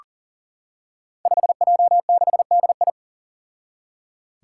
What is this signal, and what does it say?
Morse code, heard as a keyed tone, about 39 wpm, 700 Hz. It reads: 5J6DI
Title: Five Challenging Letter-Number Combinations 40wpm